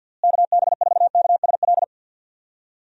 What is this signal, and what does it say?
Morse code, heard as a keyed tone, about 51 wpm, 700 Hz. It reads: KB4KSL
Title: USA Call Signs - 6 Characters - Repeated 3x - 50wpm